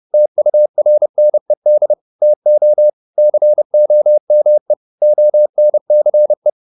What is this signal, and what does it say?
Morse code, heard as a keyed tone, about 30 wpm, 600 Hz. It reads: TURNED TO COME ONCE